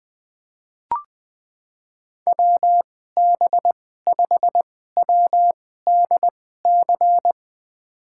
Morse code, heard as a keyed tone, about 20 wpm, spaced as for 14 wpm, 700 Hz. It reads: WB5WDC
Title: USA Call Signs - 6 Characters - 2x Character Spacing - 20wpm